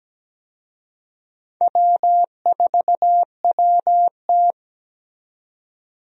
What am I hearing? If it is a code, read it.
Morse code, heard as a keyed tone, about 17 wpm, 700 Hz. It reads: W4WT
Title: USA Call Signs - 4 Characters - 17wpm